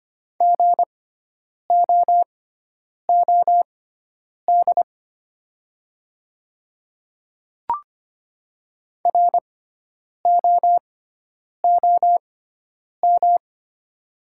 Morse code, heard as a keyed tone, about 25 wpm, spaced as for 9 wpm, 700 Hz. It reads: GOOD ROOM
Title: 6x Character Spacing - Top 300 Words - 25wpm